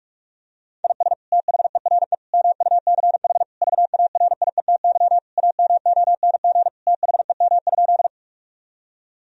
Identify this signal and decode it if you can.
Morse code, heard as a keyed tone, about 44 wpm, 700 Hz. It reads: IS THERE MUCH VARIETY AMONG THEM?
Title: Sentences from Top 1000 Words - Review Part 4 - 45wpm